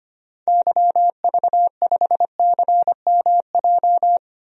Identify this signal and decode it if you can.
Morse code, heard as a keyed tone, about 25 wpm, 700 Hz. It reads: YV5CMJ
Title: DXCC-100 Call Signs - 6 Characters - 25wpm